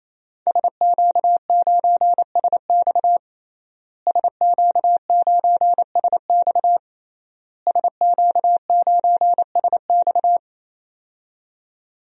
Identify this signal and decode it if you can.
Morse code, heard as a keyed tone, about 27 wpm, 700 Hz. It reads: SQ9SX SQ9SX SQ9SX
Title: DXCC-100 Call Signs - 5 Characters - Repeated 3x - 28wpm